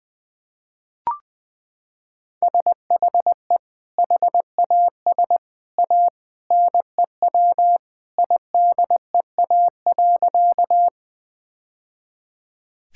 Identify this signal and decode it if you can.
Morse code, heard as a keyed tone, about 21 wpm, 700 Hz. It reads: SHE HAS A NEW IDEA.
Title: Sentences with 2-Word Phrases - 20wpm